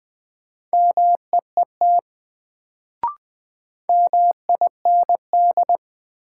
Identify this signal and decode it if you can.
Morse code, heard as a keyed tone, about 20 wpm, 700 Hz. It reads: MEET MIND
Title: Sets of 5 Words - New Word Drill - 20wpm